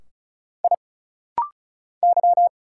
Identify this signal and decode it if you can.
Morse code, heard as a keyed tone, about 34 wpm, 700 Hz. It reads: I Y
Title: Single Letters - Rapid-Fire 35wpm